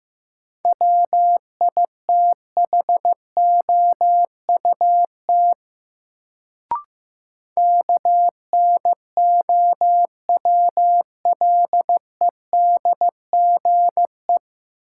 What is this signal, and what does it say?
Morse code, heard as a keyed tone, about 15 wpm, 700 Hz. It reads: WITHOUT KNOWLEDGE
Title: Sentences from Top 600 Words - New Word Drill - 15wpm